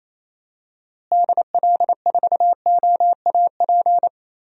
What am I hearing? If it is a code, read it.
Morse code, heard as a keyed tone, about 27 wpm, 700 Hz. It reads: DL4OAP
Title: USA Call Signs - 6 Characters - Repeated 3x - 28wpm